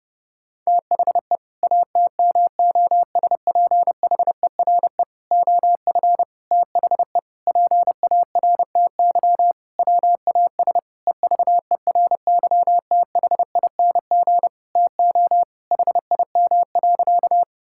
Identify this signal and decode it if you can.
Morse code, heard as a keyed tone, about 30 wpm, 700 Hz. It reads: THE ATMOSPHERE OF THE PARTY WAS EVERYTHING TO HIM.
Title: Sentences from Top 1600 Words 30wpm